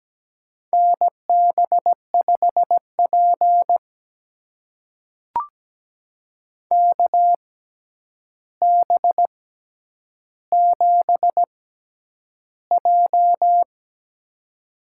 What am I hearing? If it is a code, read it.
Morse code, heard as a keyed tone, about 17 wpm, spaced as for 9 wpm, 700 Hz. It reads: NB5P KB7J
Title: USA Call Signs - 4 Characters - 6x Character Spacing - 17wpm